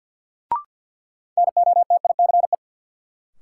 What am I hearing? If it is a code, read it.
Morse code, heard as a keyed tone, about 52 wpm, 700 Hz. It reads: NOTICE
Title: Sentences from Top 400 Words - New Word Drill - 50wpm